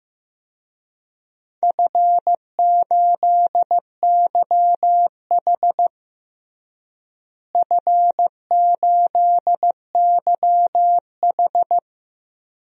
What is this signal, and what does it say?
Morse code, heard as a keyed tone, about 15 wpm, 700 Hz. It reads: F8YH F8YH
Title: DXCC-100 Call Signs - 4 Characters - Repeated 3x - 15wpm